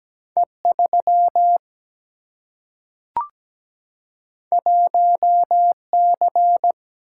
Morse code, heard as a keyed tone, about 17 wpm, 700 Hz. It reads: E 3 1 C
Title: Single-Letter Single-Number 17wpm